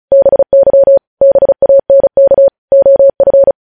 Morse code, heard as a keyed tone, about 36 wpm, 550 Hz. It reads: BY BANK OF
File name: podcast / News headlines-1x-35x35